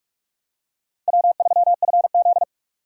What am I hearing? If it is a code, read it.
Morse code, heard as a keyed tone, about 46 wpm, 700 Hz. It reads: W3FZ
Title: USA Call Signs - 4x Character Spacing - 45wpm